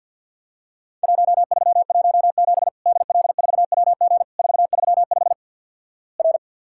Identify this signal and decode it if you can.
Morse code, heard as a keyed tone, about 50 wpm, 700 Hz. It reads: 1317 DL4WG 435 R
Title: Worked All Europe DX Contest - QTC Exchange with Repeats - 50wpm